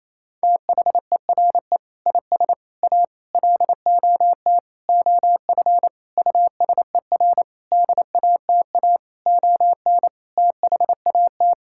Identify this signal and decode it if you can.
Morse code, heard as a keyed tone, about 28 wpm, 700 Hz. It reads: THERE IS A LOT OF USER DATA ON THAT
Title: Sentences from Top 1000 Words - Review Part 6 - 28wpm